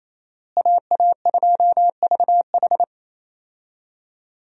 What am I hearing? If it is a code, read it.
Morse code, heard as a keyed tone, about 28 wpm, 700 Hz. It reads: AA2VH
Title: USA Call Signs - 5 Characters - Repeated 3x - 28wpm